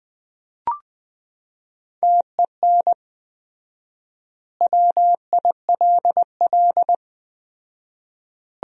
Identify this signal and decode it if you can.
Morse code, heard as a keyed tone, about 20 wpm, 700 Hz. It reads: TEN WILL
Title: 4x Word-Spacing - Sentences from Top 400 Words - 20wpm